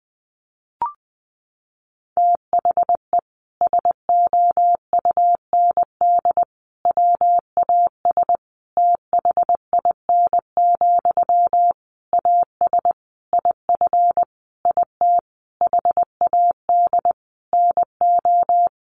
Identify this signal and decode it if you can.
Morse code, heard as a keyed tone, about 20 wpm, 700 Hz. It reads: THE SOUND WAS THIN, AS IF IT HAD NO
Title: Sentences from Top 1200 Words 20wpm